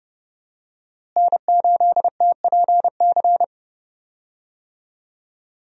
Morse code, heard as a keyed tone, about 30 wpm, 700 Hz. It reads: N8TPC
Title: USA Call Signs - 5 Characters - Repeated 2x - 30wpm